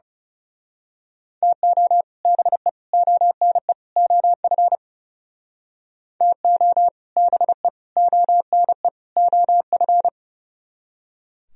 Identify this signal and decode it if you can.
Morse code, heard as a keyed tone, about 32 wpm, 700 Hz. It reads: TO BE ONE OF TO BE ONE OF
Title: Speed-Racing - Sets of 4 Words 30wpm